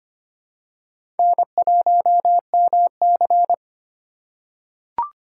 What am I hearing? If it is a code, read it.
Morse code, heard as a keyed tone, about 25 wpm, 700 Hz. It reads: N1MC
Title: USA Call Signs - 4 Characters - 4x Character Spacing - 25wpm